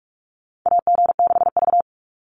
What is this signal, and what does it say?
Morse code, heard as a keyed tone, about 44 wpm, 690 Hz. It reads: AG6V
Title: NA SOTA Chaser Callsigns 45wpm